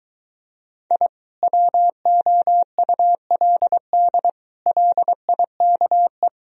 Morse code, heard as a keyed tone, about 23 wpm, 700 Hz. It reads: I WOULD LIKE